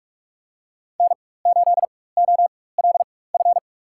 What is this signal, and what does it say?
Morse code, heard as a keyed tone, about 43 wpm, spaced as for 21 wpm, 700 Hz. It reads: N8OLF
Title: USA Call Signs - 4x Character Spacing - 45wpm